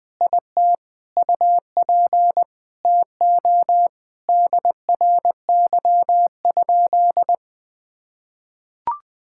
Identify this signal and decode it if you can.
Morse code, heard as a keyed tone, about 20 wpm, 700 Hz. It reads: IT UP TO DRY?
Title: Sentences from Top 1600 Words 20wpm